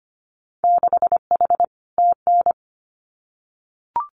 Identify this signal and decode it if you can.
Morse code, heard as a keyed tone, about 25 wpm, 700 Hz. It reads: 6H TN
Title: Winter Field Day Exchanges Encore 25wpm